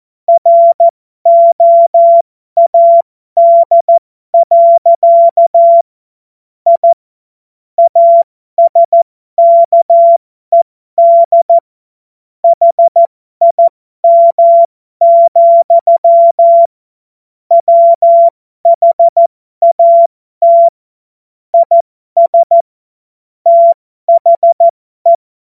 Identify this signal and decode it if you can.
Morse code, heard as a keyed tone, about 14 wpm, spaced as for 12 wpm, 680 Hz. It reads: ROAD. I ASKED HIM, WHAT IS THE
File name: Morsekurs / Witze